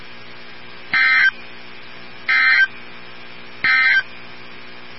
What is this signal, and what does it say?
Emergency Alert System (SAME) tones, heard as harsh buzzing bursts. It is heard from 0.8 to 4.1 s.
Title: NOAA Weather Radio 162.550